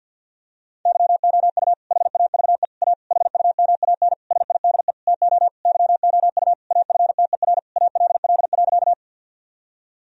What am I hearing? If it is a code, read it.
Morse code, heard as a keyed tone, about 50 wpm, 700 Hz. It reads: YOU HAVE A HUMAN SIDE TO YOU AFTER ALL.
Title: Sentences from Top 1000 Words - Review Part 3 - 50wpm